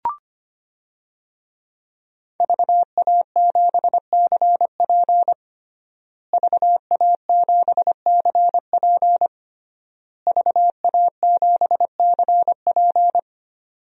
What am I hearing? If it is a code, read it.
Morse code, heard as a keyed tone, about 25 wpm, 700 Hz. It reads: VA7CP VA7CP VA7CP
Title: DXCC-100 Call Signs - 5 Characters - Repeated 3x - 25wpm